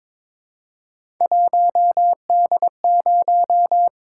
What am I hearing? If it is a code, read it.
Morse code, heard as a keyed tone, about 22 wpm, 700 Hz. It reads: 1D0
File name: practice / 22wpm / Three Letter-Number Combinations - 22wpm